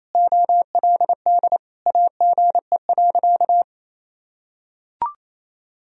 Morse code, heard as a keyed tone, about 28 wpm, 700 Hz. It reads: OLD AGE.
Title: Sentences from Top 1000 Words - Review Part 3 - 28wpm